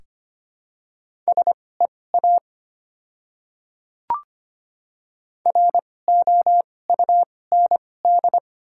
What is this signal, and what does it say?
Morse code, heard as a keyed tone, about 25 wpm, spaced as for 18 wpm, 700 Hz. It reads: SEA ROUND
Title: 2x Character Spacing - Top 300 Words - 25wpm